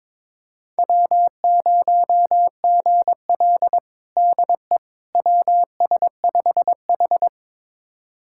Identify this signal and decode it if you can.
Morse code, heard as a keyed tone, about 21 wpm, 700 Hz. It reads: W0GL DE WS5H
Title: USA Callsign DE Callsign - 4 Characters - 22wpm